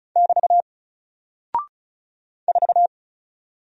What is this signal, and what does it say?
Morse code, heard as a keyed tone, about 36 wpm, 700 Hz. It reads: <BT> 4
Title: Single Character (Letter:Number:Punctuation:Prosign) Alt 35wpm